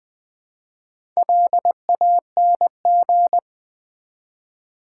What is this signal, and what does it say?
Morse code, heard as a keyed tone, about 20 wpm, 700 Hz. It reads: LANG
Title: Build A Word - 2 Letters At A Time - 8-Letter Words - 20wpm